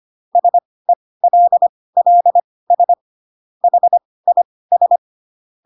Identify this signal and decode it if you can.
Morse code, heard as a keyed tone, about 25 wpm, spaced as for 18 wpm, 700 Hz. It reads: SELLS HIS